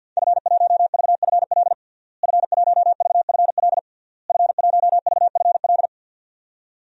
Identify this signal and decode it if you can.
Morse code, heard as a keyed tone, about 47 wpm, 700 Hz. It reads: F1VFL F1VFL F1VFL
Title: DXCC-100 Call Signs - 5 Characters - Repeated 3x - 50wpm